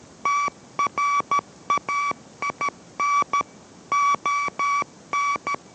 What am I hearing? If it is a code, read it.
Morse code, heard as a keyed tone, about 13 wpm, 1140 Hz. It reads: TRAIN ON